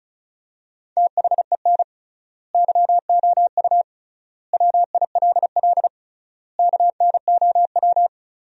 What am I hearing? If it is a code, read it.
Morse code, heard as a keyed tone, about 36 wpm, 700 Hz. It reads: THEN YOU WILL KNOW
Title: 3x Word-Spacing - Sentences from Top 400 Words - 35wpm